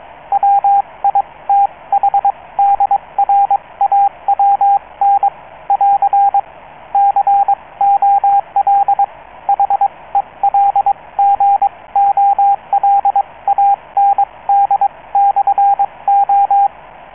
Morse code, heard as a keyed tone, about 22 wpm, spaced as for 19 wpm, 800 Hz, with radio noise. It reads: WITHDRAWN <AR> COL HELGOLAND/O